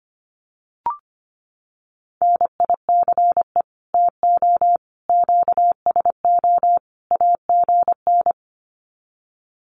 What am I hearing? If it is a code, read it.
Morse code, heard as a keyed tone, about 25 wpm, 700 Hz. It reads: NICE TO QSO AGN
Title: Common QSO Phrases 25wpm